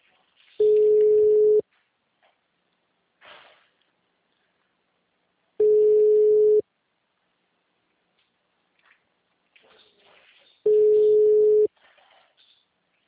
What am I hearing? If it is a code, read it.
Ringback tone (Europe, 425 Hz).